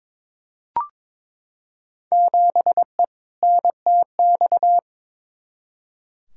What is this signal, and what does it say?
Morse code, heard as a keyed tone, about 21 wpm, 700 Hz. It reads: 7E NTX
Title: Field Day Exchanges Encore - 22wpm